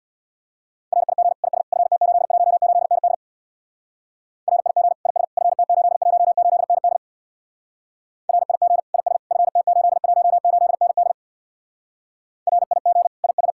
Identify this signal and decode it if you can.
Morse code, heard as a keyed tone, about 64 wpm, 700 Hz. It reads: RIG IS FT818ND RIG IS FT818ND RIG IS FT818ND RIG IS